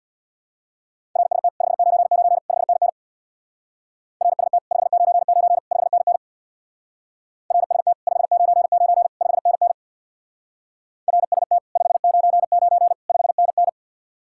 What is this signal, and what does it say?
Morse code, heard as a keyed tone, about 65 wpm, 700 Hz. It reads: RST 599 5NN RST 599 5NN RST 599 5NN RST 599 5NN